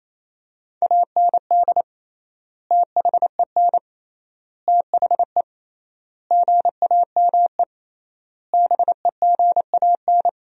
Morse code, heard as a keyed tone, about 27 wpm, 700 Hz. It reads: AND THEN THE GAME BEGAN
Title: 3x Word-Spacing - Sentences from Top 500 Words - 28wpm